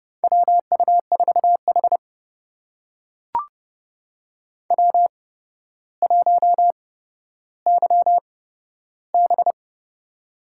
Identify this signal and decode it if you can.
Morse code, heard as a keyed tone, about 32 wpm, spaced as for 13 wpm, 700 Hz. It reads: WU4H W1YB
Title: USA Call Signs - 4 Characters - 8x Character Spacing - 30wpm